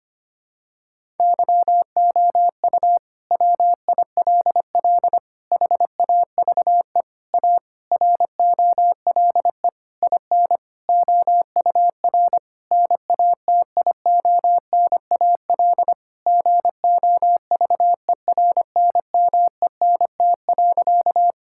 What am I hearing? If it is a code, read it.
Morse code, heard as a keyed tone, about 25 wpm, 700 Hz. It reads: YOU WILL HAVE A ROLE IN OUR NATIONAL GOVERNMENT.
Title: Sentences from Top 1000 Words - Review Part 6 - 25wpm